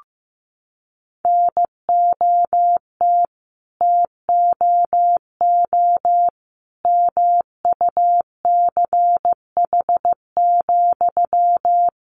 Morse code, heard as a keyed tone, about 15 wpm, 700 Hz. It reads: NOT TOO MUCH,
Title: Sentences from Top 1000 Words 15wpm